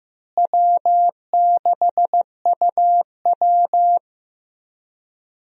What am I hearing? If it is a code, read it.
Morse code, heard as a keyed tone, about 15 wpm, 700 Hz. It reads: W6UW
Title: USA Call Signs - 4 Characters - Repeated 3x - 15wpm